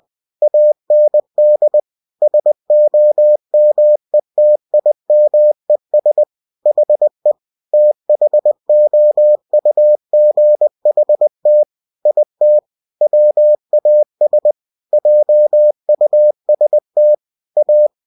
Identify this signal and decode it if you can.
Morse code, heard as a keyed tone, about 20 wpm, 600 Hz. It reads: AND SOMETIMES HE THOUGHT IT WAS JUST A
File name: cw-books / winnie-the-pooh / winnie-20wpm-page-0092